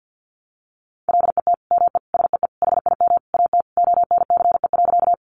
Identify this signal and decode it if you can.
Morse code, heard as a keyed tone, about 50 wpm, 700 Hz. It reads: LET ME SEE HIM AT ONCE.
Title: Sentences from Top 300 Words 50wpm